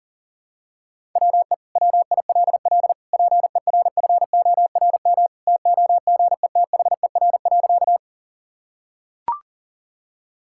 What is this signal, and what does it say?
Morse code, heard as a keyed tone, about 40 wpm, 700 Hz. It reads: WE WILL PERFORM TOGETHER.
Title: Sentences from Top 1000 Words - Review Part 5 - 40wpm